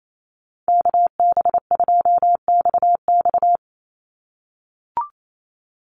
Morse code, heard as a keyed tone, about 28 wpm, 700 Hz. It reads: KB2XX
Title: NA SOTA Chaser Callsigns 28wpm